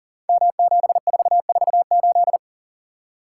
Morse code, heard as a keyed tone, about 38 wpm, 700 Hz. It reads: M7448
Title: Code Groups - 5 Letter:Numbers - 40wpm